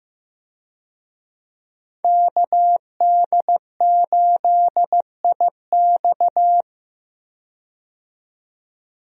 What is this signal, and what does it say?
Morse code, heard as a keyed tone, about 15 wpm, 700 Hz. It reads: KD8IX
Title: USA Call Signs - 5 Characters - 15wpm